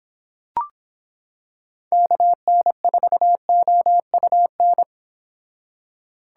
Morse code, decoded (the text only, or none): KN4OUN